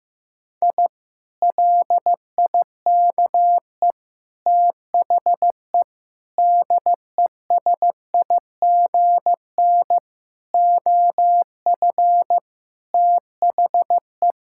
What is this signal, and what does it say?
Morse code, heard as a keyed tone, about 16 wpm, 700 Hz. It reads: I LIKE THE DESIGN OF THE
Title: Sentences from Top 1000 Words - Review Part 5 - 15wpm